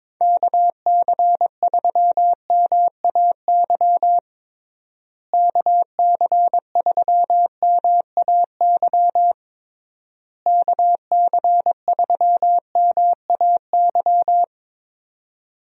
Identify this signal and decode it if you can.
Morse code, heard as a keyed tone, about 22 wpm, 700 Hz. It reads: KC3MAY KC3MAY KC3MAY
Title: USA Call Signs - 6 Characters - Repeated 3x - 22wpm